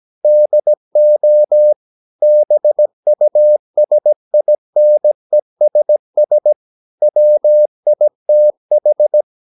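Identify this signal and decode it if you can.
Morse code, heard as a keyed tone, about 17 wpm, 600 Hz. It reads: DO BUSINESS WITH
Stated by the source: 17 wpm Morse practice sending the English text of a novel.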